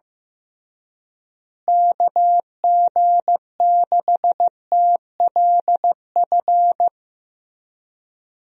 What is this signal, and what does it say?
Morse code, heard as a keyed tone, about 15 wpm, 700 Hz. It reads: KG6TLF
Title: USA Call Signs - 6 Characters - Repeated 3x - 15wpm